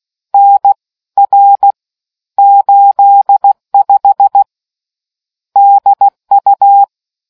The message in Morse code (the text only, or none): NR 85 DU